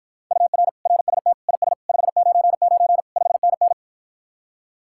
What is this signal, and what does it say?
Morse code, heard as a keyed tone, about 54 wpm, 700 Hz. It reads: UR RST IS 599 5NN